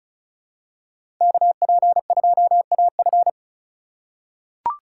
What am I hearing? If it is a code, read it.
Morse code, heard as a keyed tone, about 36 wpm, 700 Hz. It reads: KP2AF
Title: USA Call Signs - 5 Characters - 4x Character Spacing - 35wpm